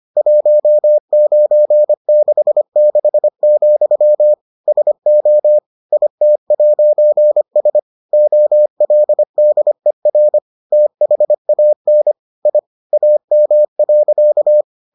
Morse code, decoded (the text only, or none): 1966, SO IT'S OLDER THAN I AM.